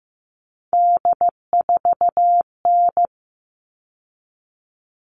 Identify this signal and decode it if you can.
Morse code, heard as a keyed tone, about 15 wpm, 700 Hz. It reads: D4N
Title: All Cut Numbers - 3-Digit 15wpm